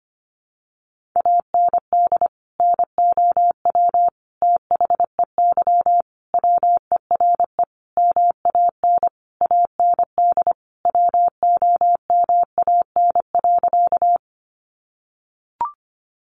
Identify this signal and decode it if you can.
Morse code, heard as a keyed tone, about 25 wpm, 700 Hz. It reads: AND NOW THEY WERE MAN AND WOMAN.
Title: Sentences from Top 1100 Words 25wpm